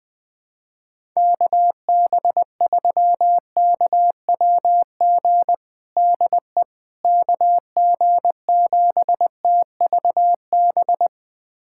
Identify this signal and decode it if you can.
Morse code, heard as a keyed tone, about 21 wpm, 700 Hz. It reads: KB3KWG DE KG7TVB
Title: USA Callsign DE Callsign 20wpm